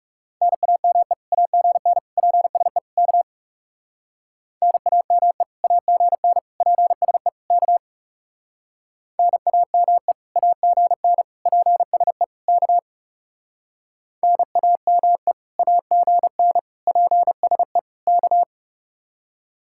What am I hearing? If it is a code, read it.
Morse code, heard as a keyed tone, about 38 wpm, 700 Hz. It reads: NAME AGN PSE K NAME AGN PSE K NAME AGN PSE K NAME AGN PSE K